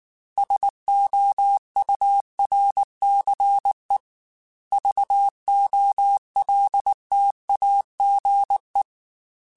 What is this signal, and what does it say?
Morse code, heard as a keyed tone, about 19 wpm, 800 Hz. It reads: SOURCE VOLTAGE
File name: Technical Subjects / Technical Subjects-mp3s / XC @18